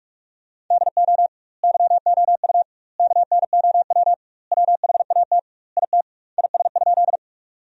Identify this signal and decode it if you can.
Morse code, heard as a keyed tone, about 46 wpm, 700 Hz. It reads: DO YOU KNOW WHAT IT IS?
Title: 2x Word-Spacing - Sentences from Top 100 Words - 45wpm